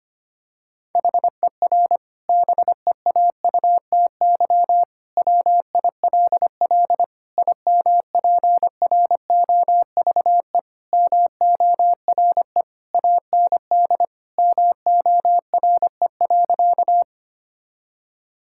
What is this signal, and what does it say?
Morse code, heard as a keyed tone, about 25 wpm, 700 Hz. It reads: HER BEAUTY WILL IMPROVE MORE AND MORE.
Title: Sentences from Top 1000 Words - Review Part 5 - 25wpm